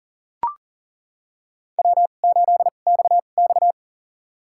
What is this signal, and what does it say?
Morse code, heard as a keyed tone, about 40 wpm, spaced as for 29 wpm, 700 Hz. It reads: W8XX
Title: USA Call Signs - 4 Characters - 2x Character Spacing - 40wpm